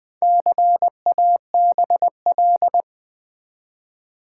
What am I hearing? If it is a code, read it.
Morse code, heard as a keyed tone, about 20 wpm, 700 Hz. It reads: CABL
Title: Build A Word - 1 Letter At A Time - 5-Letter Words - 20wpm